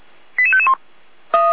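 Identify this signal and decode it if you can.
Five-tone selective call (ZVEI), heard as quick runs of five tones, tones of about 70 ms. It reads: 88551
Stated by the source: ZVEI five-tone selective call 88402, received via SDR, archived by the Feuerwehr Orlamünde fire brigade.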